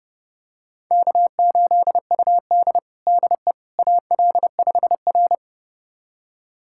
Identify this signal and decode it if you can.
Morse code, heard as a keyed tone, about 30 wpm, 700 Hz. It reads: K8UD DE AL5R
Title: USA Callsign DE Callsign - 4 Characters - 30wpm